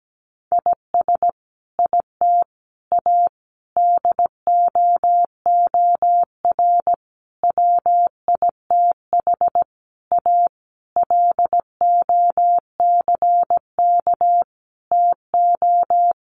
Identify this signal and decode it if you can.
Morse code, heard as a keyed tone, about 17 wpm, 700 Hz. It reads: IS IT A DOOR WITH A LOCK TO
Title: Sentences from Top 1300 Words 17wpm